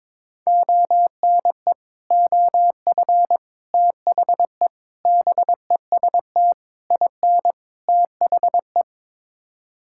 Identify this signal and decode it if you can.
Morse code, heard as a keyed tone, about 21 wpm, 700 Hz. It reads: ONE OF THE BEST IN THE
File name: practice / 22wpm / Sets of 6 Words - 22wpm